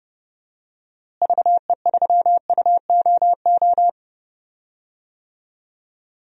Morse code, heard as a keyed tone, about 30 wpm, 700 Hz. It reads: VE3UOO
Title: DXCC-100 Call Signs - 6 Characters - Repeated 3x - 30wpm